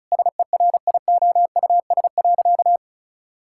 Morse code, heard as a keyed tone, about 34 wpm, 700 Hz. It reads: SERIOUS.
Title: Sentences from Top 1600 Words 35wpm